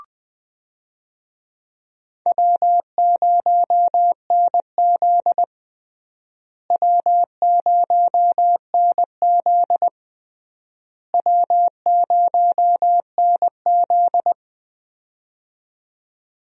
Morse code, heard as a keyed tone, about 21 wpm, 700 Hz. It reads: W0NZ W0NZ W0NZ
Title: USA Call Signs - 4 Characters - Repeated 3x - 20wpm